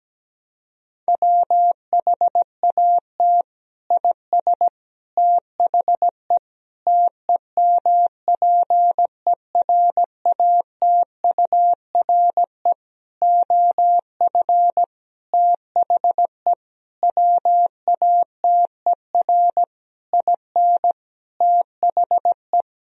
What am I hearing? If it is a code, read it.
Morse code, heard as a keyed tone, about 17 wpm, 700 Hz. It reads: WHAT IS THE TEMPERATURE OF THE WATER IN THE
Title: Sentences from Top 1000 Words - Review Part 4 - 17wpm